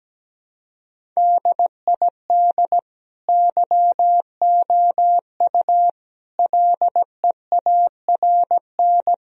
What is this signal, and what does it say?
Morse code, heard as a keyed tone, about 17 wpm, 700 Hz. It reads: DID YOU LEARN
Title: Sentences from Top 1000 Words - Review Part 1 - 17wpm